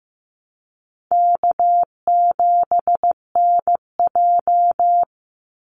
Morse code, heard as a keyed tone, about 15 wpm, 700 Hz. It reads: K7NJ
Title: Call Signs Repeated 3x - 15wpm